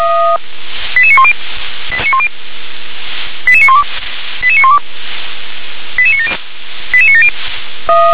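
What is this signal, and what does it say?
Five-tone selective call (ZVEI), heard as quick runs of five tones, tones of about 70 ms. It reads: 88910 88912 88912 88970 88970
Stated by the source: Five-tone ZVEI selective call received by SDR.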